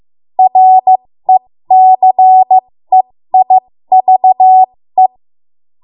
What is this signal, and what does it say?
Morse code, heard as a keyed tone, about 15 wpm, spaced as for 13 wpm, 750 Hz. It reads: RECEIVE
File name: Morse / 13WPM